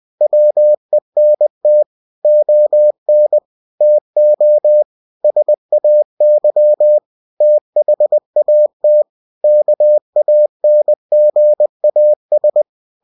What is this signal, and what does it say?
Morse code, heard as a keyed tone, about 20 wpm, 600 Hz. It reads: WENT ON TO SAY THAT KANGAS
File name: cw-books / winnie-the-pooh / winnie-20wpm-page-0062